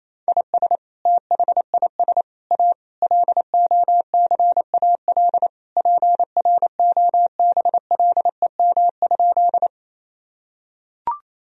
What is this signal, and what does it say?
Morse code, heard as a keyed tone, about 28 wpm, 700 Hz. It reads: IS THIS A LOCAL PROBLEM?
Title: Sentences from Top 1000 Words - Review Part 3 - 28wpm